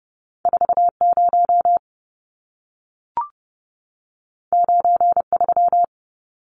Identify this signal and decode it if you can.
Morse code, heard as a keyed tone, about 30 wpm, 700 Hz. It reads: 40 93
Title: Numbers - 2-Digit 30wpm